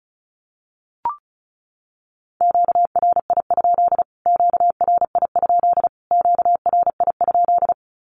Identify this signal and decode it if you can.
Morse code, heard as a keyed tone, about 34 wpm, 700 Hz. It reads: QRI? QRI? QRI?
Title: Q-Codes Repeated 3x 35wpm